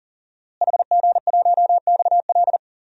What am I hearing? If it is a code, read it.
Morse code, heard as a keyed tone, about 40 wpm, 700 Hz. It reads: HG1XL
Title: DXCC-100 Call Signs - 5 Characters - Repeated 2x - 40wpm